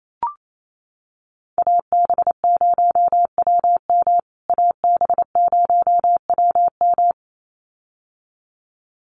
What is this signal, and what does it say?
Morse code, heard as a keyed tone, about 28 wpm, 700 Hz. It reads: AB0WM AB0WM
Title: USA Call Signs Repeated 2x 28wpm